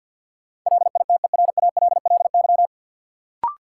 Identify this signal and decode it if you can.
Morse code, heard as a keyed tone, about 47 wpm, 700 Hz. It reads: LITERALLY
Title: Sentences from Top 1400 Words - New Word Drill Alt - 50wpm